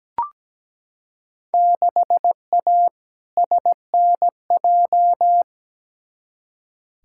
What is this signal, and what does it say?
Morse code, heard as a keyed tone, about 17 wpm, 700 Hz. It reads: 6A SNJ
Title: Field Day Exchanges Encore - 17wpm